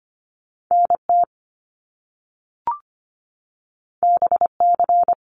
Morse code, heard as a keyed tone, about 25 wpm, 700 Hz. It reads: NT BC
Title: Canadian Provinces and Territories 25wpm